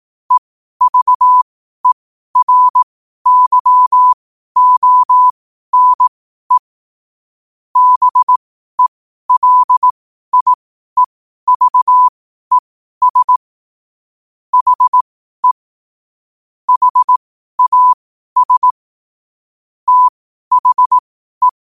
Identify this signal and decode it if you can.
Morse code, heard as a keyed tone, about 18 wpm, spaced as for 12 wpm, 1000 Hz. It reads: EVERYONE BELIEVES HE HAS THE
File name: qotdfiles / 12wpm